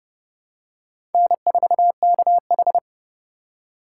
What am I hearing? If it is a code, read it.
Morse code, heard as a keyed tone, about 30 wpm, 700 Hz. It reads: N4KH
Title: K1USN SST QSO Practice - 30wpm